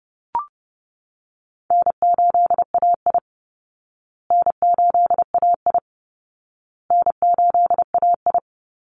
Morse code, heard as a keyed tone, about 30 wpm, 700 Hz. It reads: N8AI N8AI N8AI
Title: Call Signs Repeated 2x - 30wpm